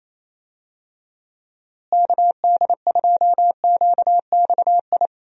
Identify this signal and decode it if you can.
Morse code, heard as a keyed tone, about 28 wpm, 700 Hz. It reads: KD2QXI
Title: USA Call Signs - 6 Characters - Repeated 3x - 28wpm